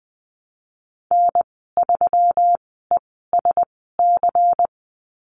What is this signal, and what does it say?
Morse code, heard as a keyed tone, about 20 wpm, spaced as for 15 wpm, 700 Hz. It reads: N3ESC
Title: 245 Lesson 39g - Letter z - Call Signs 20wpm